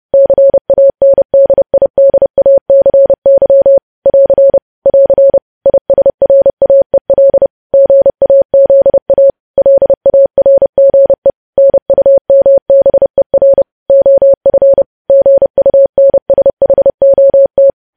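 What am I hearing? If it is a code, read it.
Morse code, heard as a keyed tone, about 30 wpm, 550 Hz. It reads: CANDIDACY <AR> <AR> ISRAEL GAZA LARGE NUMBER OF GUNSHOT
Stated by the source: Morse code practice of news headlines at 30 wpm, preceded by vvv.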